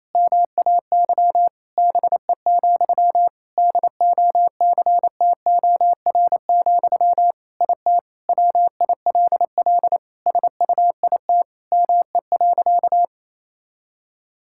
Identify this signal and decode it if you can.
Morse code, heard as a keyed tone, about 28 wpm, 700 Hz. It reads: MAY BE, DOCTOR, IT WILL SUIT ME.
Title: Sentences from Top 1600 Words 28wpm